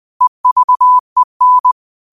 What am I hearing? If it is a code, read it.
Morse code, heard as a keyed tone, about 20 wpm, 1000 Hz. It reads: EVEN
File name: qotdfiles / 20wpm